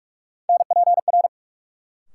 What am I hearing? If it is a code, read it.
Morse code, heard as a keyed tone, about 42 wpm, 700 Hz. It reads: NPR
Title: Sets of 5 Words - New Word Drill - 45wpm